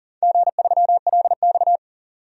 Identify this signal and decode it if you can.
Morse code, heard as a keyed tone, about 40 wpm, 700 Hz. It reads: G3LX
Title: DXCC-100 Call Signs - 4 Characters - Repeated 3x - 40wpm